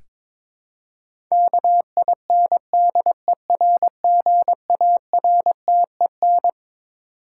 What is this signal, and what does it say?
Morse code, heard as a keyed tone, about 22 wpm, 700 Hz. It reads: KINDERGARTEN
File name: Twelve Letter Words - 22wpm